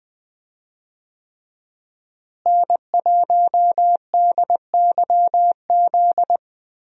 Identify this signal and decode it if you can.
Morse code, heard as a keyed tone, about 20 wpm, 700 Hz. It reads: N1DYZ
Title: USA Call Signs - 5 Characters - Repeated 3x - 20wpm